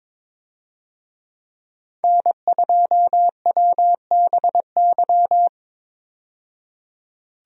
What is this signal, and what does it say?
Morse code, heard as a keyed tone, about 22 wpm, 700 Hz. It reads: N2WBY
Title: USA Call Signs - 5 Characters - 22wpm